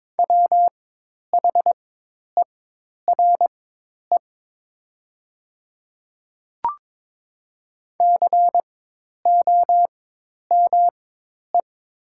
Morse code, heard as a keyed tone, about 22 wpm, spaced as for 10 wpm, 700 Hz. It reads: WHERE COME
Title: 4x Character Spacing - Top 300 Words - 22wpm